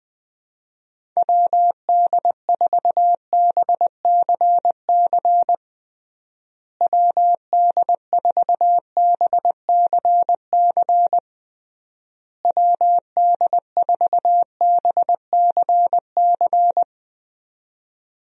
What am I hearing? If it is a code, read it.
Morse code, heard as a keyed tone, about 21 wpm, 700 Hz. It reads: WD4BCC WD4BCC WD4BCC
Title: USA Call Signs - 6 Characters - Repeated 3x - 20wpm